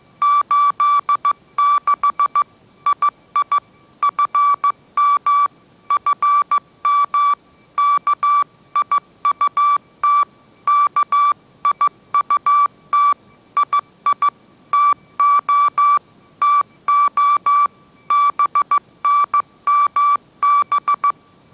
Morse code, heard as a keyed tone, about 15 wpm, 1160 Hz. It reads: 86 II FM FM KIUT KIUT II TO TO BNMB
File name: mrx / mrx066 16wpm